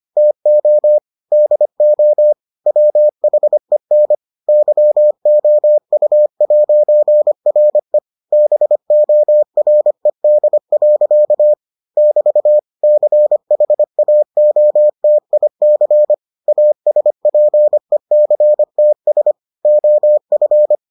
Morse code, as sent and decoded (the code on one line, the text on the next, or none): - --- / -.. --- / .-- .... -. / -.-- --- ..- .----. .-. / -... --- .-. -.. .-.-.- / -...- / -.-. .... .- --- - .. -.-. / .- ... .--. -.-. - ... / --- ..-.
TO DO WHEN YOU'RE BORED. <BT> CHAOTIC ASPECTS OF